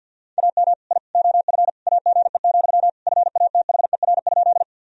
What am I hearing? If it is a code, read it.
Morse code, heard as a keyed tone, about 52 wpm, 700 Hz. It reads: AM I OF AGE, FATHER?
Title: Sentences from Top 1000 Words - Review Part 3 - 50wpm